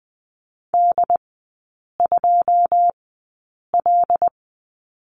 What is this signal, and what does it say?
Morse code, heard as a keyed tone, about 20 wpm, spaced as for 8 wpm, 700 Hz. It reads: D2L
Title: 081 Lesson 14e - Number 2 - Three Characters 20wpm